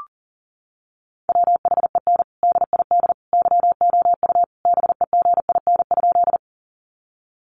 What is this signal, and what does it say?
Morse code, heard as a keyed tone, about 40 wpm, 700 Hz. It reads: WHEN DID YOU BEGIN?
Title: Sentences from Top 300 Words 40wpm